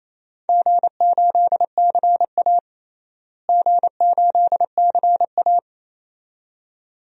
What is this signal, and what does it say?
Morse code, heard as a keyed tone, about 28 wpm, 700 Hz. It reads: G8CA G8CA
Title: DXCC-100 Call Signs - 4 Characters - Repeated 2x - 28wpm